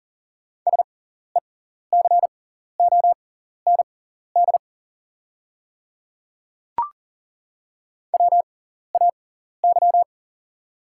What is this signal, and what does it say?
Morse code, heard as a keyed tone, about 40 wpm, spaced as for 14 wpm, 700 Hz. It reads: SECOND WAY
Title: 6x Character Spacing - Top 300 Words - 40wpm